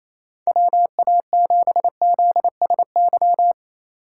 Morse code, heard as a keyed tone, about 34 wpm, 700 Hz. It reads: WA7ZSY
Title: Speed Racing - USA 2x3 Call Signs 28wpm